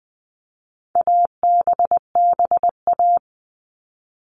Morse code, heard as a keyed tone, about 20 wpm, 700 Hz. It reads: ABBA
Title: All Cut Numbers - 4-Digit 20wpm